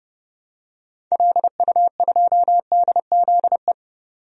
Morse code, heard as a keyed tone, about 30 wpm, 700 Hz. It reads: LU2DZE
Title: DXCC-100 Call Signs - 6 Characters - 30wpm